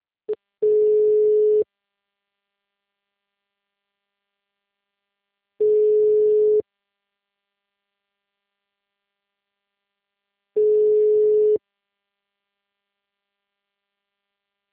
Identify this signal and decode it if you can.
Ringback tone (Europe, 425 Hz).